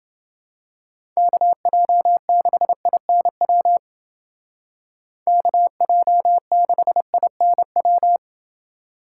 Morse code, heard as a keyed tone, about 29 wpm, 700 Hz. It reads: KJ6INW KJ6INW